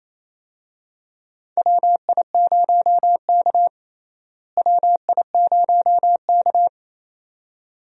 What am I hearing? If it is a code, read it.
Morse code, heard as a keyed tone, about 28 wpm, 700 Hz. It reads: WI0K WI0K
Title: USA Call Signs - 4 Characters - Repeated 2x - 28wpm